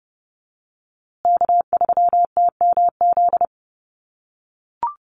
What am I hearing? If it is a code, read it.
Morse code, heard as a keyed tone, about 30 wpm, 700 Hz. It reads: K3TMZ
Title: USA Call Signs Repeated 2x 30wpm